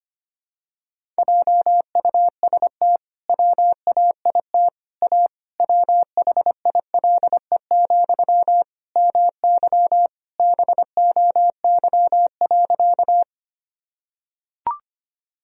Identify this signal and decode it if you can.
Morse code, heard as a keyed tone, about 25 wpm, 700 Hz. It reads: JUST WAIT A WHILE, MY BOY.
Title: Sentences from Top 1000 Words - Review Part 1 - 25wpm